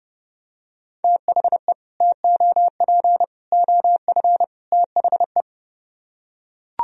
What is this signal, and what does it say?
Morse code, heard as a keyed tone, about 32 wpm, 700 Hz. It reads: THE TOP OF THE